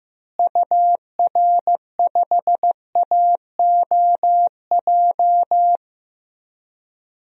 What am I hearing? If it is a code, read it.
Morse code, heard as a keyed tone, about 16 wpm, 700 Hz. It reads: UR5AOJ
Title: DXCC-100 Call Signs - 6 Characters - Repeated 2x - 15wpm